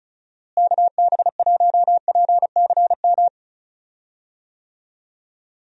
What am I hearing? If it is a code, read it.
Morse code, heard as a keyed tone, about 34 wpm, 700 Hz. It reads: KB1PCM
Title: USA Call Signs - 6 Characters - Repeated 3x - 35wpm